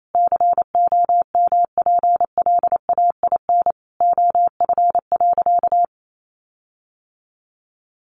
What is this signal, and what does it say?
Morse code, heard as a keyed tone, about 28 wpm, 700 Hz. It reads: COMPLAIN OF.
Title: Sentences from Top 1300 Words 28wpm